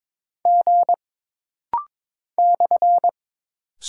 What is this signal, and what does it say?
Morse code, heard as a keyed tone, about 22 wpm, 700 Hz. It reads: G /
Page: This rapid-fire timing is geared to push your ICR (Instant Character Recognition) skills.